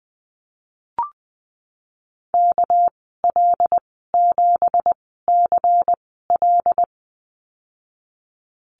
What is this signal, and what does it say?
Morse code, heard as a keyed tone, about 19 wpm, spaced as for 14 wpm, 700 Hz. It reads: KL7CL
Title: 207 Lesson 33g - Letter j - Call Signs 20wpm